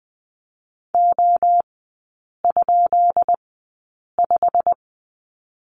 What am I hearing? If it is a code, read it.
Morse code, heard as a keyed tone, about 20 wpm, spaced as for 8 wpm, 690 Hz. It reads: O?5
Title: 133 Lesson 22e - Question Mark - Three Characters 20wpm